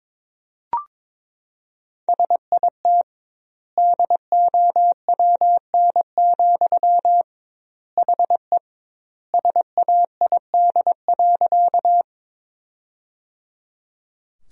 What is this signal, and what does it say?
Morse code, heard as a keyed tone, about 22 wpm, 700 Hz. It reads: SIT DOWN, HE SAID.
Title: 2x Word-Spacing - Sentences from Top 500 Words - 22wpm